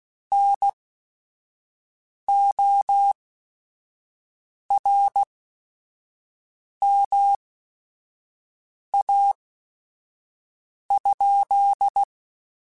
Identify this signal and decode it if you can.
Morse code, heard as a keyed tone, about 16 wpm, spaced as for 5 wpm, 800 Hz. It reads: NORMA?
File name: QSOs / QSOs-mp3s / WD0FBT 1@5